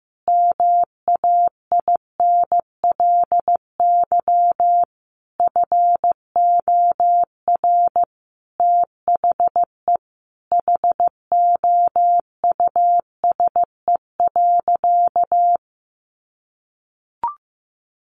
Morse code, heard as a keyed tone, about 15 wpm, 700 Hz. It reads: MAINLY FOR THE HOUSE.
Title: Sentences from Top 1400 Words 15wpm